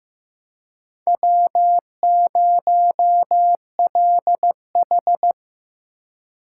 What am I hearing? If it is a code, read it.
Morse code, heard as a keyed tone, about 18 wpm, 700 Hz. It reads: W0LH
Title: Speed Racing - USA 1x2 Call Signs 15wpm